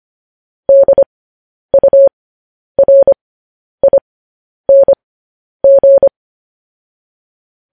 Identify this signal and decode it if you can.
Morse code, heard as a keyed tone, about 25 wpm, spaced as for 10 wpm, 550 Hz. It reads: DURING
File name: podcast / News headlines-1x-25x10